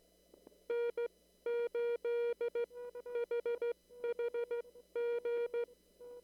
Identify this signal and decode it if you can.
Morse code, heard as a keyed tone, about 15 wpm, spaced as for 12 wpm, 470 Hz. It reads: N8HHG
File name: CW / kt5j wpx cw-op-k5pi